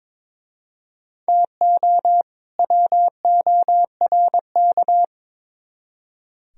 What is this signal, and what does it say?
Morse code, heard as a keyed tone, about 26 wpm, 700 Hz. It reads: TO WORK
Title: Speed-Racing - Sets of 2 Words 22wpm